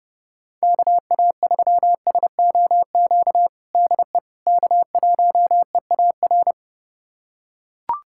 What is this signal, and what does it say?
Morse code, heard as a keyed tone, about 30 wpm, 700 Hz. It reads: KA3SOQ DE K1EAR
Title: USA Callsign DE Callsign 30wpm